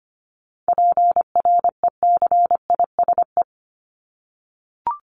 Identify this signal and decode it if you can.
Morse code, heard as a keyed tone, about 25 wpm, 700 Hz. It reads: PRECISE
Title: LY Suffix - Top 100 Words with Base Words 25wpm